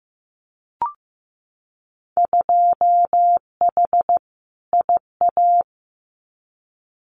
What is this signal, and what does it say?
Morse code, heard as a keyed tone, about 15 wpm, 700 Hz. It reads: 2H IA
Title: Winter Field Day Exchanges Encore 15wpm